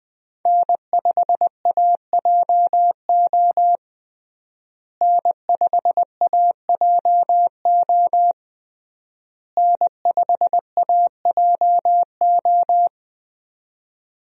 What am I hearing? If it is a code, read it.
Morse code, heard as a keyed tone, about 20 wpm, 700 Hz. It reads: N5AJO N5AJO N5AJO
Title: USA Call Signs - 5 Characters - Repeated 3x - 20wpm